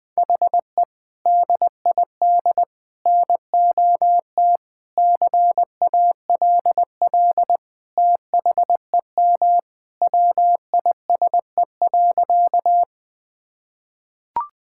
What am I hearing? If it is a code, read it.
Morse code, heard as a keyed tone, about 20 wpm, 700 Hz. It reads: HE DID NOT CALL THEM WISE.
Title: Sentences from Top 1600 Words 20wpm